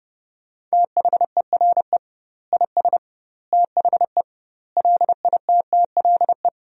Morse code, heard as a keyed tone, about 30 wpm, 700 Hz. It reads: THERE IS THE LITTLE
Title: 2x Word-Spacing - Sentences from Top 200 Words - 30wpm